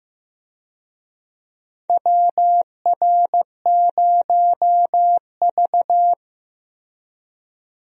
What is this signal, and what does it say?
Morse code, heard as a keyed tone, about 15 wpm, 700 Hz. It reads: WR0V
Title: USA Call Signs - 4 Characters - Repeated 2x - 15wpm